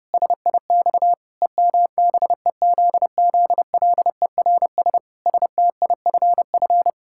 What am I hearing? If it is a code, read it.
Morse code, heard as a keyed tone, about 30 wpm, 700 Hz. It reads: SIX EMBEZZLERS STIFF
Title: Pangram Challenge v2 - 30wpm